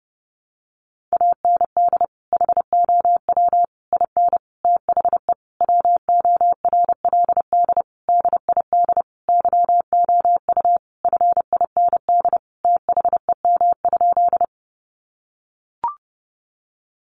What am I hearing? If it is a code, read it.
Morse code, heard as a keyed tone, about 30 wpm, 700 Hz. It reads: AND HOW IN THE WORLD DID YOU FIND THEM?
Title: Sentences from Top 200 Words 30wpm